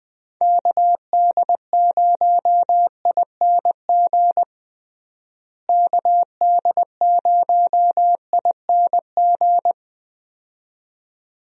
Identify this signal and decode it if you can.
Morse code, heard as a keyed tone, about 20 wpm, 700 Hz. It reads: KD0ING KD0ING
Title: USA Call Signs - 6 Characters - Repeated 2x - 20wpm